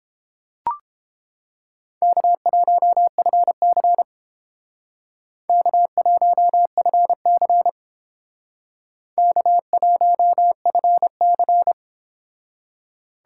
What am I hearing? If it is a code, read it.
Morse code, heard as a keyed tone, about 30 wpm, 700 Hz. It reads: K1FC K1FC K1FC